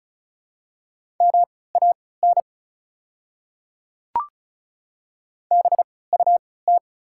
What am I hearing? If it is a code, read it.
Morse code, heard as a keyed tone, about 34 wpm, spaced as for 20 wpm, 700 Hz. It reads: MAN BUT
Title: 3x Character Spacing - Top 300 Words - 35wpm